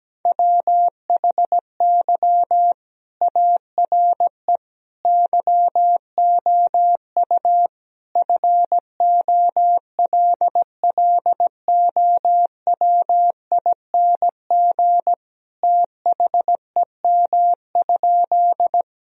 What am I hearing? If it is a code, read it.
Morse code, heard as a keyed tone, about 17 wpm, 700 Hz. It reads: WHY ARE YOU FOLLOWING THEM?
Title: Sentences from Top 1000 Words - Review Part 4 - 17wpm